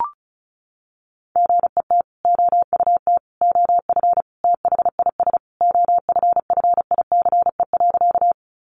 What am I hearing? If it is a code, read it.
Morse code, heard as a keyed tone, about 34 wpm, 700 Hz. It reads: GET OUT OF THIS OFFICE.
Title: Sentences from Top 1000 Words 35wpm